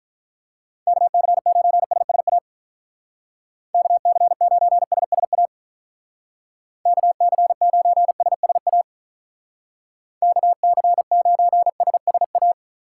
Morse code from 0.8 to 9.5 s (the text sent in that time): KC9SSA KC9SSA KC9SSA